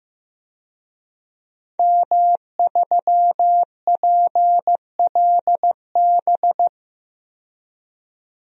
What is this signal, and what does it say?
Morse code, heard as a keyed tone, about 15 wpm, 700 Hz. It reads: M3PLB
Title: DXCC-100 Call Signs - 5 Characters - 15wpm